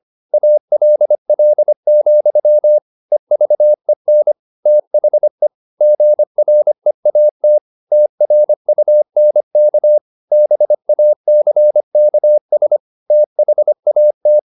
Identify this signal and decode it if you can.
Morse code, heard as a keyed tone, about 25 wpm, 600 Hz. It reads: ALL, EVEN THE GREAT TRUNK BACKS THAT
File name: the old man and the sea part 1 25wpm